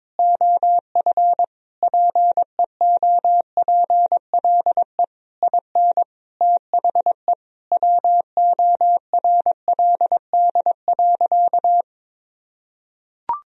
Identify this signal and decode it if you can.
Morse code, heard as a keyed tone, about 22 wpm, 700 Hz. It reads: OF PEOPLE IN THE WORLD.
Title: Sentences from Top 1000 Words - Review Part 4 - 22wpm